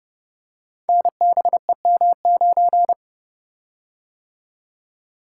Morse code, heard as a keyed tone, about 30 wpm, 700 Hz. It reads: NBEM9
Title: Code Groups - 5 Letter:Numbers - 30wpm